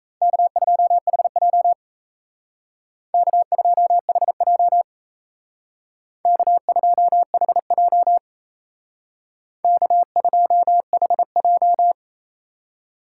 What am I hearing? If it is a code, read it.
Morse code, heard as a keyed tone, about 36 wpm, 700 Hz. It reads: K2HJ K2HJ K2HJ K2HJ